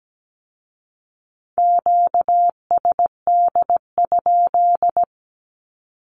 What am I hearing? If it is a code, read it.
Morse code, heard as a keyed tone, about 17 wpm, 700 Hz. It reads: QSD?
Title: Q-Codes with Spoken Meaning - 17wpm